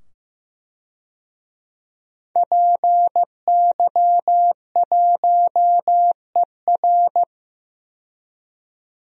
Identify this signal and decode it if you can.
Morse code, heard as a keyed tone, about 15 wpm, 700 Hz. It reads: PY1ER
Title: DXCC-100 Call Signs - 5 Characters - 15wpm